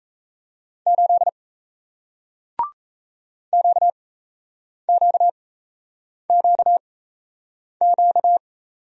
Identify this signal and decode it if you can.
Morse code, heard as a keyed tone, about 37 wpm, spaced as for 8 wpm, 700 Hz. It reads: 8 QQQQ